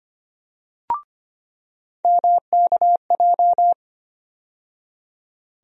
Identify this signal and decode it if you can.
Morse code, heard as a keyed tone, about 25 wpm, 700 Hz. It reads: MKJ
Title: USA Call Sign Postfixes - 3 Letters 25wpm